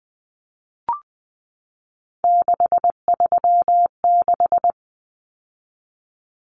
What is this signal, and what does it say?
Morse code, heard as a keyed tone, about 20 wpm, 700 Hz. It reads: 636
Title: Common Cut Numbers - 3-Digit 20wpm